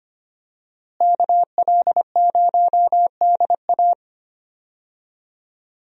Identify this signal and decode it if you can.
Morse code, heard as a keyed tone, about 25 wpm, 700 Hz. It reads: KL0DA
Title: USA Call Signs - 5 Characters - Repeated 2x - 25wpm